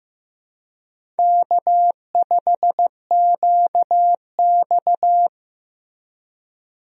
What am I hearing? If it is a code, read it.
Morse code, heard as a keyed tone, about 15 wpm, 700 Hz. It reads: K5QX
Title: USA Call Signs - 4 Characters - Repeated 3x - 15wpm